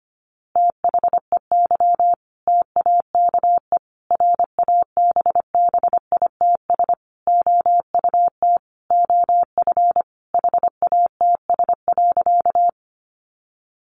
Morse code, heard as a keyed tone, about 25 wpm, 700 Hz. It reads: THEY TAKE RABBITS OUT OF HATS.
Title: Easy Phrases 25wpm